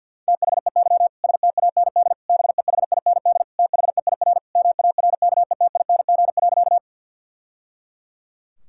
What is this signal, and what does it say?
Morse code, heard as a keyed tone, about 50 wpm, 700 Hz. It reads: THEY STAND BEHIND THEIR MARKETING.
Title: Sentences from Top 1000 Words - Review Part 5 - 50wpm